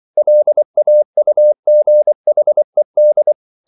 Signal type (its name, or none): Morse code